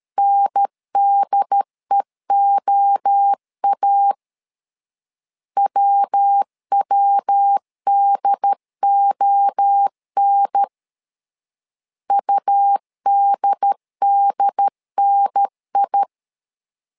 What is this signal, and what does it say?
Morse code, heard as a keyed tone, about 13 wpm, 800 Hz. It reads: NDEOA WWDON UDDNI
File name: LEZIONI / 3 MWRO / 3 MWRO 12wpm 1730